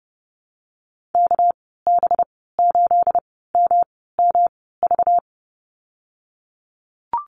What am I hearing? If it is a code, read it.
Morse code, heard as a keyed tone, about 30 wpm, spaced as for 17 wpm, 700 Hz. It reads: KB8MMV
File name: Lesson 37g - Letter x - Call Signs 30wpm